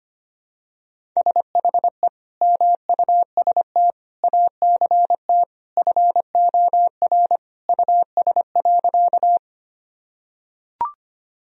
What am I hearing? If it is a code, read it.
Morse code, heard as a keyed tone, about 25 wpm, 700 Hz. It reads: SHE MUST ACT FOR US.
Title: Sentences from Top 1000 Words - Review Part 1 - 25wpm